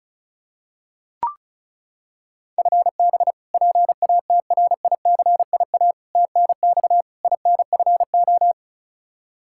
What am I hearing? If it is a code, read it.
Morse code, heard as a keyed tone, about 34 wpm, 700 Hz. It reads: FB PATRICIA TNX INFO
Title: Practice QSO - Exchange 3 - More Info - 35wpm